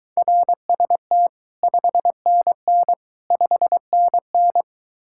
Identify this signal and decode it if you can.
Morse code, heard as a keyed tone, about 23 wpm, 700 Hz. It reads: RST 5NN 5NN